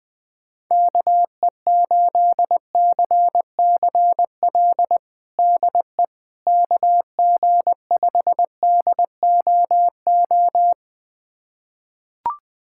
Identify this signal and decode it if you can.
Morse code, heard as a keyed tone, about 20 wpm, 700 Hz. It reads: KE8CCL DE KG5DOO
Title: USA Callsign DE Callsign - 6 Characters - 20wpm